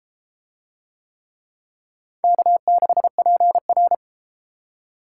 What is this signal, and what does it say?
Morse code, heard as a keyed tone, about 32 wpm, 700 Hz. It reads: K6PR